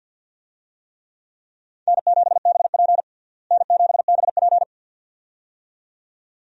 Morse code, heard as a keyed tone, about 49 wpm, 700 Hz. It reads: N7BP N7BP
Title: USA Call Signs - 4 Characters - Repeated 2x - 50wpm